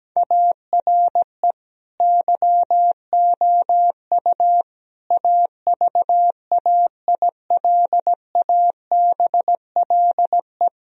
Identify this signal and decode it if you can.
Morse code, heard as a keyed tone, about 17 wpm, 700 Hz. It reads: ARE YOU AVAILABLE
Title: Sentences from Top 1000 Words - Review Part 3 - 17wpm